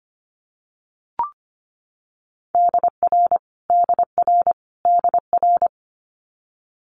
Morse code, heard as a keyed tone, about 25 wpm, 700 Hz. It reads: DR DR DR
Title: Multi-letter Phonemes Repeated 3x 25wpm